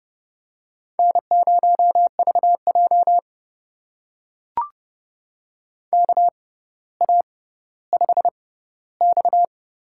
Morse code, heard as a keyed tone, about 30 wpm, spaced as for 15 wpm, 700 Hz. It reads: N0VJ KA5X
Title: USA Call Signs - 4 Characters - 6x Character Spacing - 30wpm